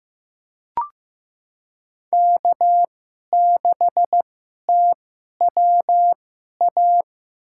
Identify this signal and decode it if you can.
Morse code, heard as a keyed tone, about 15 wpm, spaced as for 11 wpm, 700 Hz. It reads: K6TWA
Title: USA Call Signs - 2x Character Spacing - 15wpm